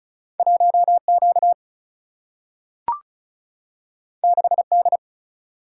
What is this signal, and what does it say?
Morse code, heard as a keyed tone, about 36 wpm, 700 Hz. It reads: 1Q 6D
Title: Single-Letter Single-Number 35wpm